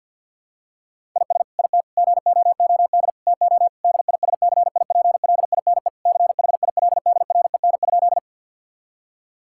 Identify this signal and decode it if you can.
Morse code, heard as a keyed tone, about 50 wpm, 700 Hz. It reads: IS IT GOOD TO DISCIPLINE CHILDREN?
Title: Sentences from Top 1000 Words - Review Part 6 - 50wpm